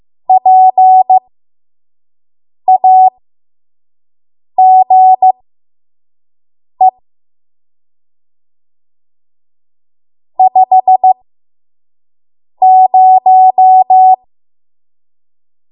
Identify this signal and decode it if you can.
Morse code, heard as a keyed tone, about 15 wpm, spaced as for 5 wpm, 750 Hz. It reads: PAGE 50
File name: Morse / 05WPM